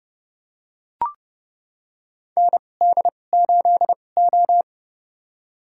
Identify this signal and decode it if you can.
Morse code, heard as a keyed tone, about 30 wpm, spaced as for 22 wpm, 700 Hz. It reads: ND8O
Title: USA Call Signs - 2x Character Spacing - 30wpm